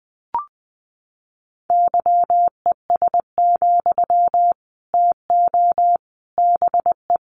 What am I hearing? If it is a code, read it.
Morse code, heard as a keyed tone, about 20 wpm, 700 Hz. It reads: YES, TO BE
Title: Sentences from Top 500 Words 20wpm